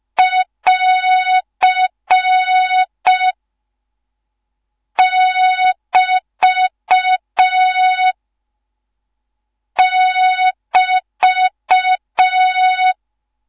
Morse code, heard as a keyed tone, about 5 wpm, 3700 Hz. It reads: <AR> <BT> <BT>
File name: Morze-HA2MN / hang / 13IRASJELEK1 HA2MN